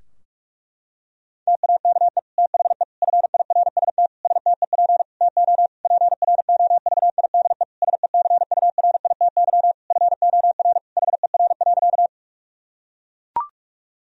Morse code, heard as a keyed tone, about 41 wpm, 700 Hz. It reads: TAKE THE FIRST STEP TO PROVIDE SECURITY FOR HER.
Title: Sentences from Top 1000 Words - Review Part 4 - 45wpm